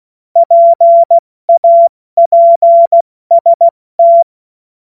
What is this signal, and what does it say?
Morse code, heard as a keyed tone, about 16 wpm, spaced as for 14 wpm, 680 Hz. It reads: PAPST
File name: Morsekurs / Witze / Witz 16. Februar 2021, 70 F80